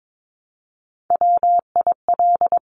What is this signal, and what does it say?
Morse code, heard as a keyed tone, about 22 wpm, 700 Hz. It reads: WIL
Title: Top Three Letter Combinations in Words 22wpm